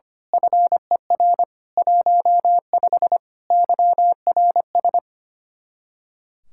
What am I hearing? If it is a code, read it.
Morse code, heard as a keyed tone, about 24 wpm, 700 Hz. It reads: FER 15 YRS
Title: Speed-Racing Common QSO Phrases - 25wpm